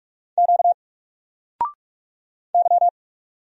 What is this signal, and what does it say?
Morse code, heard as a keyed tone, about 46 wpm, 700 Hz. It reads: Q Y
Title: Single Letter-Number 45wpm